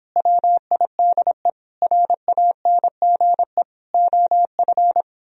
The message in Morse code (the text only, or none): WIDE RANGE OF